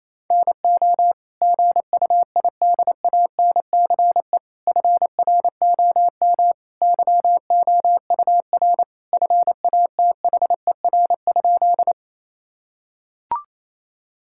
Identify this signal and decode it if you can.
Morse code, heard as a keyed tone, about 27 wpm, 700 Hz. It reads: NO GUIDANCE FROM YOUR FATHER?
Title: Sentences from Top 1600 Words 28wpm